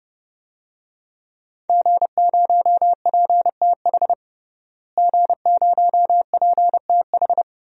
Morse code, heard as a keyed tone, about 32 wpm, 700 Hz. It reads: G0PTH G0PTH
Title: DXCC-100 Call Signs - 5 Characters - Repeated 2x - 30wpm